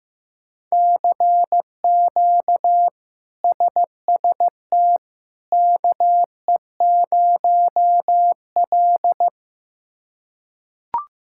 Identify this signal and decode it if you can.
Morse code, heard as a keyed tone, about 15 wpm, 700 Hz. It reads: CQ SST KE0L
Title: K1USN SST QSO Practice - 15wpm